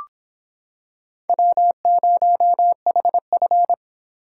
Morse code, heard as a keyed tone, about 26 wpm, 700 Hz. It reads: W0HF